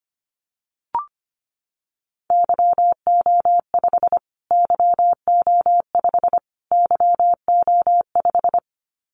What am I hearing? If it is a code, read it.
Morse code, heard as a keyed tone, about 25 wpm, 700 Hz. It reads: YO5 YO5 YO5
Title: Callsign Prefixes for Top 25 DXCC - Repeated 3x - 25wpm